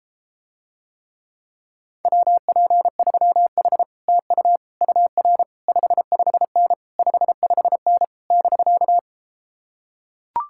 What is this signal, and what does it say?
Morse code, heard as a keyed tone, about 33 wpm, 700 Hz. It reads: WP3H TU UR 55N 55N <BK>